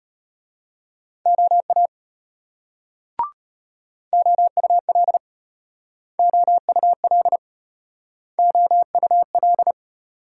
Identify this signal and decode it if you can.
Morse code, heard as a keyed tone, about 35 wpm, 700 Hz. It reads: OA OUL OUL OUL